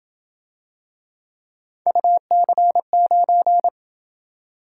Morse code, heard as a keyed tone, about 26 wpm, 700 Hz. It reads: UC9